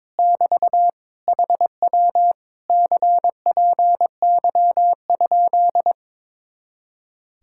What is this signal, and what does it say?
Morse code, heard as a keyed tone, about 22 wpm, 700 Hz. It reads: <BT> HW CPY?
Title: Speed-Racing Common QSO Phrases Encore - 22wpm